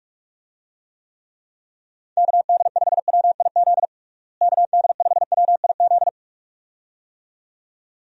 Morse code, heard as a keyed tone, about 41 wpm, 700 Hz. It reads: KD5WIZ KD5WIZ
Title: USA Call Signs - 6 Characters - Repeated 2x - 45wpm